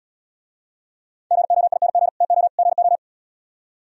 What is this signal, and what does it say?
Morse code, heard as a keyed tone, about 77 wpm, 700 Hz. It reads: GOING TO GO